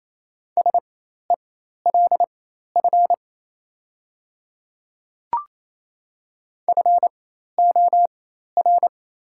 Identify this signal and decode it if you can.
Morse code, heard as a keyed tone, about 28 wpm, spaced as for 13 wpm, 690 Hz. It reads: SELF FOR
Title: 4x Character Spacing - Top 200 Words - 28wpm